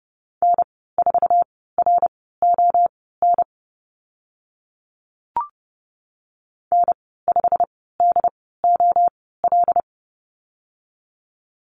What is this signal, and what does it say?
Morse code, heard as a keyed tone, about 30 wpm, spaced as for 17 wpm, 690 Hz. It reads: N4RON N5DOL
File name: Lesson 15g - Number 5 - Call Signs 30wpm